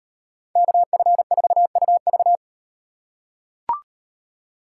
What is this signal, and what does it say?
Morse code, heard as a keyed tone, about 36 wpm, 700 Hz. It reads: KF4UV